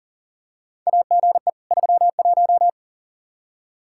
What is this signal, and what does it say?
Morse code, heard as a keyed tone, about 40 wpm, 700 Hz. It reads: AGE 31
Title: K1USN SST QSO Practice 40wpm